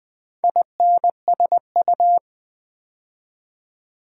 Morse code, heard as a keyed tone, about 20 wpm, 700 Hz. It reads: INSU
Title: Build A Word - 4 Letters At A Time - 12-Letter Words - 20wpm